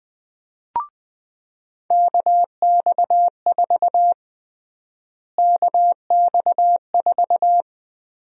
Morse code, heard as a keyed tone, about 20 wpm, 700 Hz. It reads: KX4 KX4
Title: USA Call Sign Prefixes - Repeated 2x - 20wpm